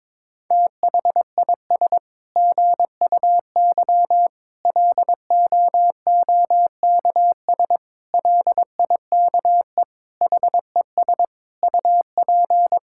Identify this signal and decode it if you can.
Morse code, heard as a keyed tone, about 22 wpm, 700 Hz. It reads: THIS GUY LOOKS LIKE HES UP
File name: Sets of 9 Words - 22wpm